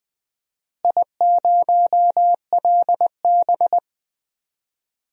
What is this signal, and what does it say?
Morse code, heard as a keyed tone, about 20 wpm, 700 Hz. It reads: I0LB
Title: DXCC-100 Call Signs - 4 Characters - Repeated 2x - 20wpm